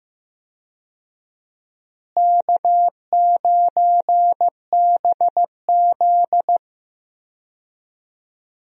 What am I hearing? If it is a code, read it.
Morse code, heard as a keyed tone, about 15 wpm, 700 Hz. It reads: K9BZ
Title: USA Call Signs - 4 Characters - Repeated 3x - 15wpm